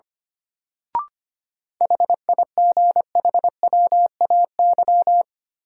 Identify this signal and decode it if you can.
Morse code, heard as a keyed tone, about 25 wpm, 700 Hz. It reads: HIGHWAY
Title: Sentences from Top 1400 Words - New Word Drill - 25wpm